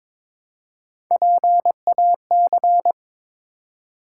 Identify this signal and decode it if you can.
Morse code, heard as a keyed tone, about 22 wpm, 700 Hz. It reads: PAC
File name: ARRL Sections v2 - 22wpm